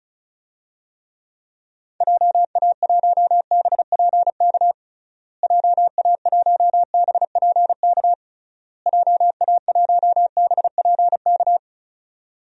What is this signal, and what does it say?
Morse code, heard as a keyed tone, about 36 wpm, 700 Hz. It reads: JA1BPK JA1BPK JA1BPK
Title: DXCC-100 Call Signs - 6 Characters - Repeated 3x - 35wpm